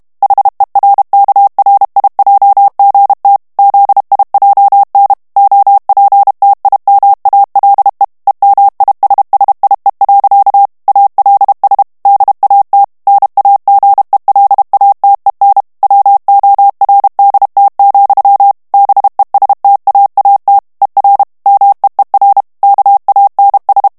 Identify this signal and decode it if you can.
Morse code, heard as a keyed tone, about 32 wpm, 800 Hz. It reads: HERKRIJGT ZIJN OPTIMALE EMISSIE. ALS DAT NAGELATEN WORDT, BESTAAT ER MEER KANS
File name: morse mp3 / HOLLANDAIS texte clair 2 vitesse 30 mots minute